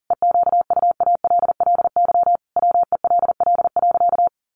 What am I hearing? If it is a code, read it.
Morse code, heard as a keyed tone, about 40 wpm, 700 Hz. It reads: EQUALLY WELL.
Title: Sentences from Top 1400 Words 40wpm